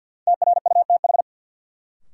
Morse code, heard as a keyed tone, about 52 wpm, 700 Hz. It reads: TRUTH
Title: Sets of 5 Words - New Word Drill - 50wpm